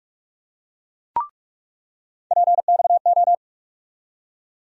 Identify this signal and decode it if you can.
Morse code, heard as a keyed tone, about 41 wpm, 700 Hz. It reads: PXO
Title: USA Call Sign Postfixes - 3 Letters 45wpm